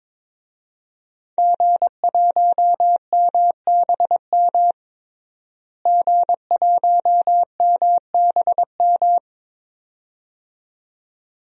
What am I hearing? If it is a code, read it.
Morse code, heard as a keyed tone, about 22 wpm, 700 Hz. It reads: G1MBM G1MBM
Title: DXCC-100 Call Signs - 5 Characters - Repeated 2x - 22wpm